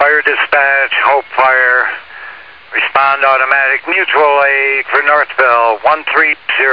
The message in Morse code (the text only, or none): none